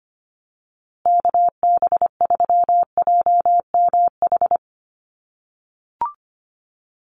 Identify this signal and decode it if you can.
Morse code, heard as a keyed tone, about 25 wpm, 700 Hz. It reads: KB3JMH
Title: USA Call Signs Repeated 3x 25wpm